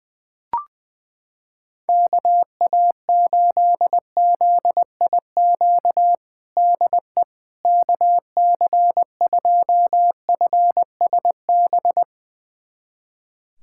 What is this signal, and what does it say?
Morse code, heard as a keyed tone, about 20 wpm, 700 Hz. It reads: KA8ZIQ DE KC2FSB
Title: USA Callsign DE Callsign 20wpm